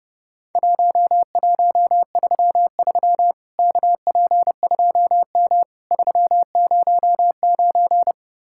Morse code, heard as a keyed tone, about 30 wpm, 700 Hz. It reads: 1133 KP2M 309
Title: Worked All Europe DX Contest - QTC Exchange - 30wpm